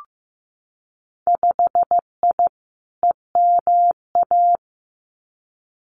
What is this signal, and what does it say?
Morse code, heard as a keyed tone, about 15 wpm, 700 Hz. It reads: 5I EMA
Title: Winter Field Day Exchanges Encore 15wpm